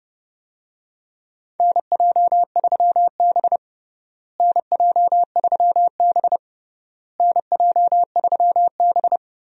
Morse code, heard as a keyed tone, about 30 wpm, 700 Hz. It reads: NJ3B NJ3B NJ3B
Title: USA Call Signs - 4 Characters - Repeated 3x - 30wpm